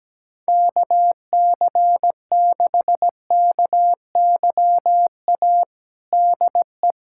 Morse code, heard as a keyed tone, about 17 wpm, 700 Hz. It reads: KC6KYA DE
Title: USA Callsign DE Callsign 17wpm